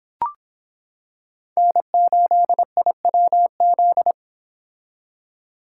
Morse code, heard as a keyed tone, about 26 wpm, 700 Hz. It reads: N8IWZ